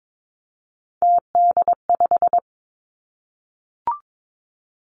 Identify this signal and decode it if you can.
Morse code, heard as a keyed tone, about 22 wpm, 700 Hz. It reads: TD5
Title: Three Challenging Letter-Number Combinations 22wpm